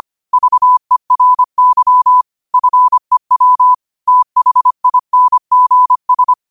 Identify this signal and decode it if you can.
Morse code, heard as a keyed tone, about 25 wpm, 1000 Hz. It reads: VERY FEW THINGS
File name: qotdfiles / 25wpm